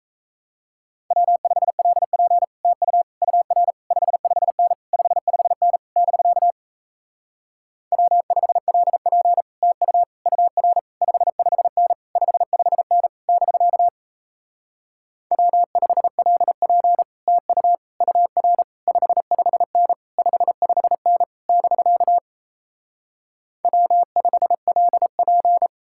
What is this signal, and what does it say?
Morse code, heard as a keyed tone, about 37 wpm, 700 Hz. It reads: W5LP TU UR 55N 55N <BK> W5LP TU UR 55N 55N <BK> W5LP TU UR 55N 55N <BK> W5LP